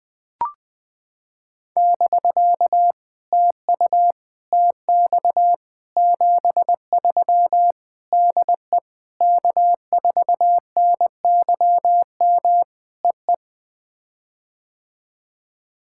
Morse code, heard as a keyed tone, about 18 wpm, 700 Hz. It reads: <BK> TU TX 73 DE K4NYM EE
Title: POTA QSO Practice - 20wpm